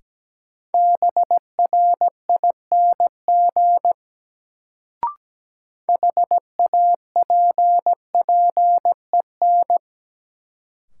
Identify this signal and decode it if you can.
Morse code, heard as a keyed tone, about 17 wpm, 700 Hz. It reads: BRING HAPPEN
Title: Sets of 5 Words Encore - New Word Drill - 17wpm